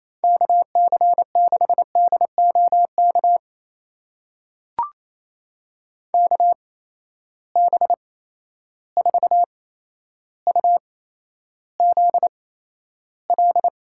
Morse code, heard as a keyed tone, about 28 wpm, spaced as for 12 wpm, 700 Hz. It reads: KC6DOK KB4UZL
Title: USA Call Signs - 6 Characters - 8x Character Spacing - 28wpm